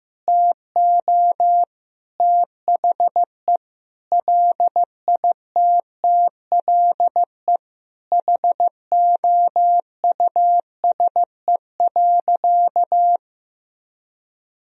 Morse code, heard as a keyed tone, about 15 wpm, 700 Hz. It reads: TO THE LITTLE HOUSE.
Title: Sentences from Top 200 Words and Abbreviations 15wpm